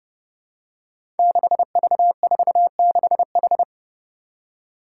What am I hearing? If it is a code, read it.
Morse code, heard as a keyed tone, about 30 wpm, 700 Hz. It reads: 6V46H
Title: Five Challenging Letter-Number Combinations 30wpm